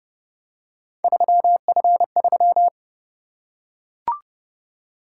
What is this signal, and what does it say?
Morse code, heard as a keyed tone, about 30 wpm, 700 Hz. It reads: 3F3